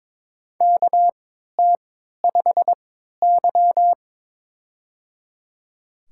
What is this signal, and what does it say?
Morse code, heard as a keyed tone, about 22 wpm, spaced as for 12 wpm, 700 Hz. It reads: KT5Y
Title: USA Call Signs - 4 Characters - 3x Character Spacing - 22wpm